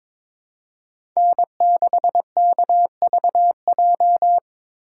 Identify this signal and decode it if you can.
Morse code, heard as a keyed tone, about 21 wpm, 700 Hz. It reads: N6KVJ
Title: USA Call Signs - 5 Characters - 2x Character Spacing - 22wpm